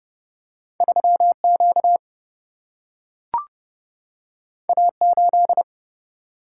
Morse code, heard as a keyed tone, about 30 wpm, 700 Hz. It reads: 3Q A8
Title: Single-Letter Single-Number 30wpm